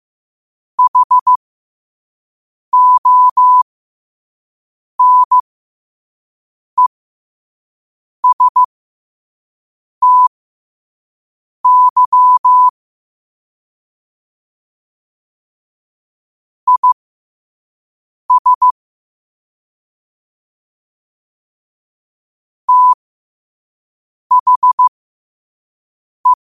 Morse code, heard as a keyed tone, about 15 wpm, spaced as for 5 wpm, 1000 Hz. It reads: HONESTY IS THE
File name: qotdfiles / 5wpm